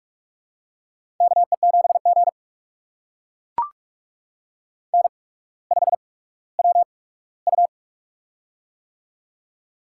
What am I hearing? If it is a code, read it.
Morse code, heard as a keyed tone, about 44 wpm, spaced as for 19 wpm, 700 Hz. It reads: KE7G N5WU
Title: USA Call Signs - 8x Character Spacing - 45wpm